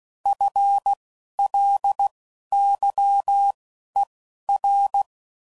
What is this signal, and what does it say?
Morse code, heard as a keyed tone, about 16 wpm, spaced as for 11 wpm, 800 Hz. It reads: FLYER